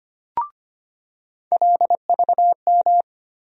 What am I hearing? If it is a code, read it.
Morse code, heard as a keyed tone, about 25 wpm, 700 Hz. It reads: LVM
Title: USA Call Sign Postfixes - 3 Letters 25wpm